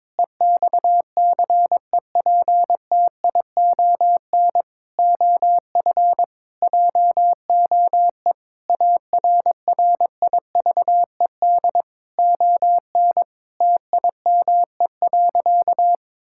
Morse code, heard as a keyed tone, about 22 wpm, 700 Hz. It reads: EXCEPTION OF JOE ARRIVED ON TIME.
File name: practice / 22wpm / Sentences with 4-Word Phrases - 22wpm